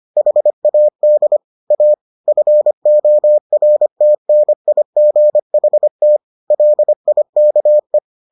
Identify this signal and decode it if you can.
Morse code, heard as a keyed tone, about 25 wpm, 600 Hz. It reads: HAD A FORTNIGHT LIKE